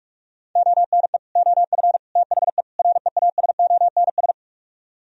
Morse code, heard as a keyed tone, about 49 wpm, 700 Hz. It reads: ONE OF THE REASONS